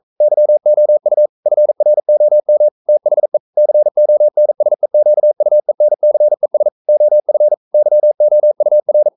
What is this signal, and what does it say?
Morse code, heard as a keyed tone, about 42 wpm, 600 Hz. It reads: YOU FROM THE CONSEQUENCES OF YOUR
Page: Romanen Never af Ken Follet på engelsk.